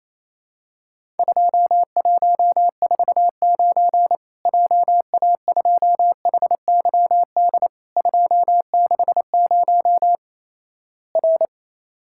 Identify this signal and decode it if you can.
Morse code, heard as a keyed tone, about 28 wpm, 700 Hz. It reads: 2149 JA2HYD 260 R
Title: Worked All Europe DX Contest - QTC Exchange - 28wpm